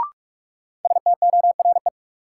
Morse code, heard as a keyed tone, about 46 wpm, 700 Hz. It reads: STORE
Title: Sentences from Top 1000 Words - Review Part 5 - New Word Drill - 45wpm